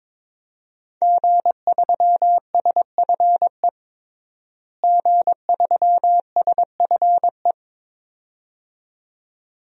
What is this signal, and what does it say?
Morse code, heard as a keyed tone, about 22 wpm, 700 Hz. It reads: G3SFE G3SFE
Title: DXCC-100 Call Signs - 5 Characters - Repeated 2x - 22wpm